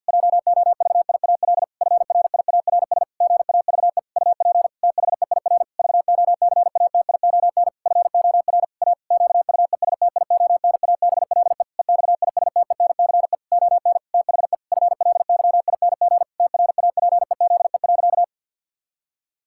Morse code, heard as a keyed tone, about 49 wpm, 700 Hz. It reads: JOVIAL FRIARS GAVE UP THEIR VOCATION FOR A QUESTIONABLE EXISTENCE ON THE FLYING TRAPEZE.
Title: Pangram Challenge v2 - 50wpm